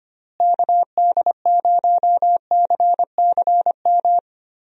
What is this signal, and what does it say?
Morse code, heard as a keyed tone, about 25 wpm, 700 Hz. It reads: KD0CCM
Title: USA Call Signs - 6 Characters - Repeated 2x - 25wpm